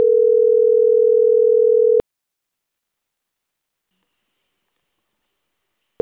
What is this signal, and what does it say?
Ringback tone (North America), 440 + 480 Hz.